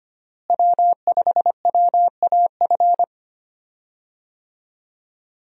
Morse code, heard as a keyed tone, about 25 wpm, 700 Hz. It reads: W5WAF
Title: USA Call Signs - 5 Characters - Repeated 3x - 25wpm